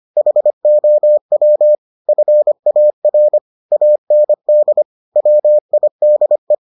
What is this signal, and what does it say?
Morse code, heard as a keyed tone, about 25 wpm, 600 Hz. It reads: HOW FAR AND WIDE